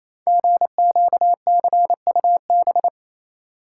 Morse code, heard as a keyed tone, about 28 wpm, 700 Hz. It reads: GQCUB
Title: Code Groups - 5 Letter:Numbers - 28wpm